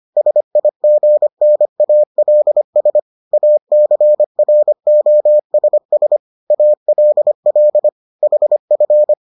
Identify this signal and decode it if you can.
Morse code, heard as a keyed tone, about 25 wpm, 600 Hz. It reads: SIGNALS ACROSS ALL HF